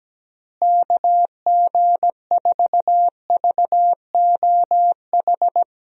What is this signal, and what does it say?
Morse code, heard as a keyed tone, about 17 wpm, 700 Hz. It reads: KG4VOH
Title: USA Call Signs - 6 Characters - Repeated 3x - 17wpm